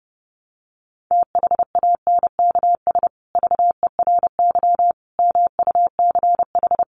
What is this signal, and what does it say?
Morse code, heard as a keyed tone, about 28 wpm, 700 Hz. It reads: THANKS VERY MUCH
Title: Sets of 5 Words Encore2 30wpm